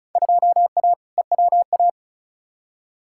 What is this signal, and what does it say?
Morse code, heard as a keyed tone, about 34 wpm, 700 Hz. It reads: 2A EWA
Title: Field Day Exchanges - 35wpm